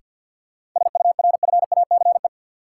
Morse code, heard as a keyed tone, about 47 wpm, 700 Hz. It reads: SURFACE
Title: Sets of 4 Words - New Word Drill - 50wpm